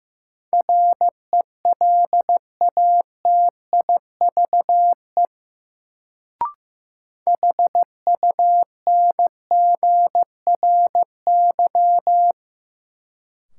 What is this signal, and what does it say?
Morse code, heard as a keyed tone, about 15 wpm, 700 Hz. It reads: RELATIVE HUNGRY
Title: Sentences from Top 1400 Words - New Word Drill Alt - 15wpm